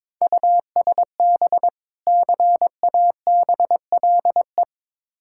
Morse code, heard as a keyed tone, about 23 wpm, 690 Hz. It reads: USB CABLE
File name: Difficult Words with Hints - 22wpm